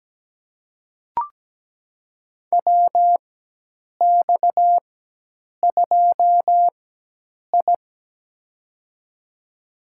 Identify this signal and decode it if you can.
Morse code, heard as a keyed tone, about 17 wpm, spaced as for 8 wpm, 700 Hz. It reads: WX2I
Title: USA Call Signs - 4 Characters - 4x Character Spacing - 17wpm